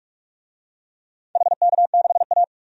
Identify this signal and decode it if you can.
Morse code, heard as a keyed tone, about 46 wpm, 700 Hz. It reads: HK6A
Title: DXCC-100 Call Signs - 4 Characters - 45wpm